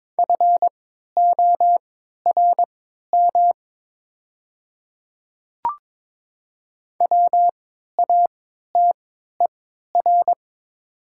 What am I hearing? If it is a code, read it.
Morse code, heard as a keyed tone, about 22 wpm, spaced as for 12 wpm, 690 Hz. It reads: FORM WATER
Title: 3x Character Spacing - Top 500 Words - 22wpm